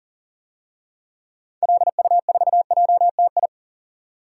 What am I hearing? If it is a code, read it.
Morse code, heard as a keyed tone, about 40 wpm, 700 Hz. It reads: LU4JTI
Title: DXCC-100 Call Signs - 6 Characters - 40wpm